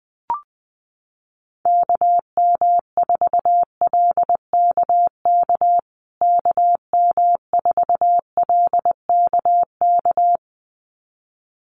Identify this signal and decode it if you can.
Morse code, heard as a keyed tone, about 20 wpm, 700 Hz. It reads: KM4LKK KM4LKK
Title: USA Call Signs Repeated 2x 20wpm